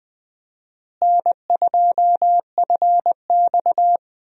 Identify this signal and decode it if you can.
Morse code, heard as a keyed tone, about 21 wpm, 700 Hz. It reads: N2FX
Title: USA Call Signs - 4 Characters - Repeated 2x - 20wpm